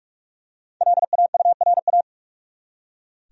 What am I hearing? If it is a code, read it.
Morse code, heard as a keyed tone, about 46 wpm, 700 Hz. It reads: LAURA
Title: USA Common Names - Max 5 Letters - 45wpm